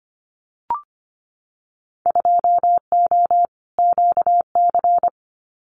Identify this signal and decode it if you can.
Morse code, heard as a keyed tone, about 25 wpm, 700 Hz. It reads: 2O QC
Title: Winter Field Day Exchanges 25wpm